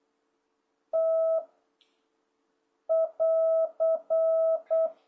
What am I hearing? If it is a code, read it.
Morse code, heard as a keyed tone, about 8 wpm, 660 Hz. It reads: T <AR>